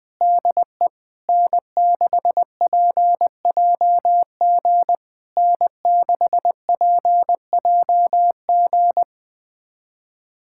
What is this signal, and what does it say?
Morse code, heard as a keyed tone, about 20 wpm, 700 Hz. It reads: DE N6PJG N6PJG
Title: SOTA QSO Practice 20wpm